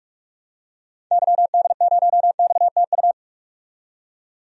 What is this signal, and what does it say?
Morse code, heard as a keyed tone, about 46 wpm, 700 Hz. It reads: YD0XTU
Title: DXCC-100 Call Signs - 6 Characters - 45wpm